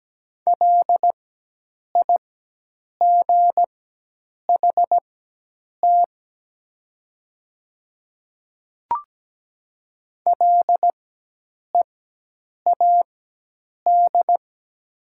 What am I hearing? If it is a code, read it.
Morse code, heard as a keyed tone, about 17 wpm, spaced as for 8 wpm, 700 Hz. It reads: LIGHT LEAD
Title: 4x Character Spacing - Top 500 Words - 17wpm